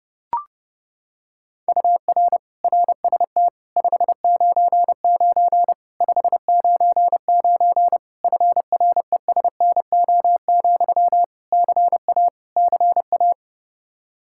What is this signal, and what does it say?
Morse code, heard as a keyed tone, about 30 wpm, 700 Hz. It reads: UR RST 599 599 FRESNO, CA CA
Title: Practice QSO - Exchange 1 - Beginning - 30wpm